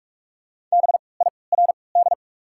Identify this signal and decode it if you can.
Morse code, heard as a keyed tone, about 45 wpm, 700 Hz. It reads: B I R D
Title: 3x Character Spacing - Top 500 Words - 45wpm